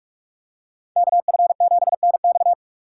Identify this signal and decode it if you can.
Morse code, heard as a keyed tone, about 44 wpm, 700 Hz. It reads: KF7NX